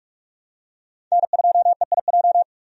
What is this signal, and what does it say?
Morse code, heard as a keyed tone, about 46 wpm, 700 Hz. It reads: N2EIJ
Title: USA Call Signs - 5 Characters - 45wpm